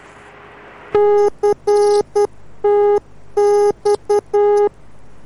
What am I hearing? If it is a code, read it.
Morse code, heard as a keyed tone, about 10 wpm, 410 Hz. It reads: CTX